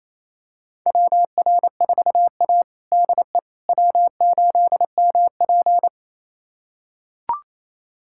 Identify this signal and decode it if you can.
Morse code, heard as a keyed tone, about 28 wpm, 700 Hz. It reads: WR4A DE W8MP
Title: USA Callsign DE Callsign - 4 Characters - 28wpm